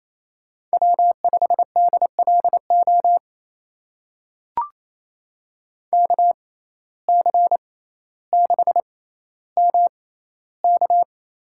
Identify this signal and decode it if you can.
Morse code, heard as a keyed tone, about 28 wpm, spaced as for 14 wpm, 700 Hz. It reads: W5DLO KC6MK
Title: USA Call Signs - 6x Character Spacing - 28wpm